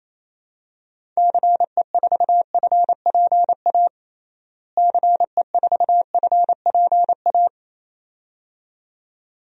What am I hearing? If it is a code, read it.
Morse code, heard as a keyed tone, about 28 wpm, 700 Hz. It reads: CE4FPA CE4FPA
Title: DXCC-100 Call Signs - 6 Characters - Repeated 2x - 28wpm